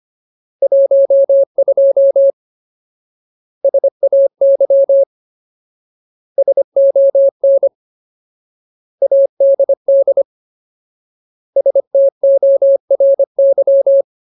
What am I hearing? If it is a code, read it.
Morse code, heard as a keyed tone, about 25 wpm, 550 Hz. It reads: 12 SAY SON ADD STORY
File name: Ultimate Word List / 25WPM Ultimate